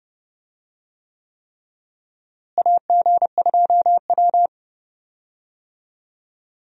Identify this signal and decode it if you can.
Morse code, heard as a keyed tone, about 30 wpm, 700 Hz. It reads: AG2W
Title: USA Call Signs - 4 Characters - Repeated 3x - 30wpm